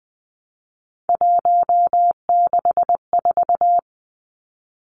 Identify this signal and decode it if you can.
Morse code, heard as a keyed tone, about 20 wpm, 700 Hz. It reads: 164
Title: Numbers - 3-Digit 20wpm